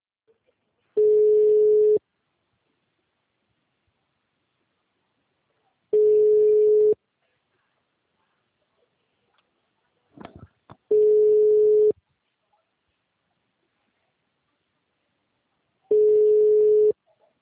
Ringback tone (Europe, 425 Hz).